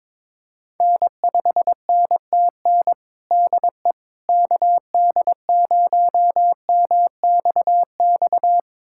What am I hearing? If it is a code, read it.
Morse code, heard as a keyed tone, about 22 wpm, 700 Hz. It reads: N5NTN DE KD0MXX
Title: USA Callsign DE Callsign 22wpm